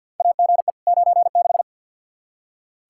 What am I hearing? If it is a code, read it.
Morse code, heard as a keyed tone, about 46 wpm, 700 Hz. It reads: AGE 96
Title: Practice QSO - Exchange 3 - More Info - 50wpm